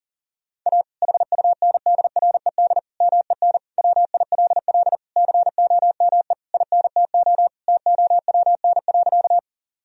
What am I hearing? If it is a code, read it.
Morse code, heard as a keyed tone, about 40 wpm, 700 Hz. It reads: A HUNDRED MEN WILL COME INTO TOWN.
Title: Sentences from Top 1000 Words - Review Part 2 - 40wpm